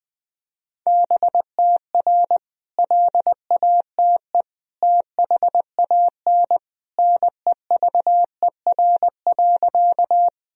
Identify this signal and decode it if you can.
Morse code, heard as a keyed tone, about 20 wpm, 700 Hz. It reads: BTR LATE THAN NEVER.
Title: Sentences from Top 400 Words and Abbreviations 20wpm